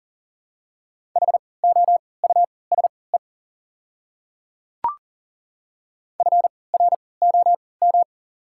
Morse code, heard as a keyed tone, about 39 wpm, spaced as for 23 wpm, 690 Hz. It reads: HOUSE FROM
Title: 3x Character Spacing - Top 200 Words - 40wpm